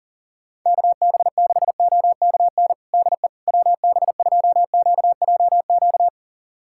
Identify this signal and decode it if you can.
Morse code, heard as a keyed tone, about 40 wpm, 700 Hz. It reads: KB6OKN DE WB2QJQ
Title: USA Callsign DE Callsign - 6 Characters - 40wpm